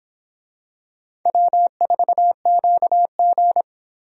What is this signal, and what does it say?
Morse code, heard as a keyed tone, about 24 wpm, 700 Hz. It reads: W4QG